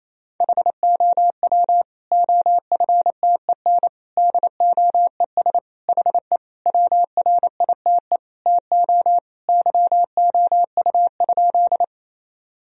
Morse code, heard as a keyed tone, about 28 wpm, 700 Hz. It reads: HOW OFTEN DOES HE WRITE TO YOU?
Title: Sentences from Top 1000 Words - Review Part 1 - 28wpm